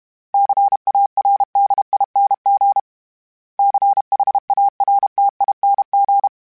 Morse code, heard as a keyed tone, about 32 wpm, 800 Hz. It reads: CARDING CHARTING
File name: Special INGs-mp3s / Special INGs-mp3s / Sngl Ing @30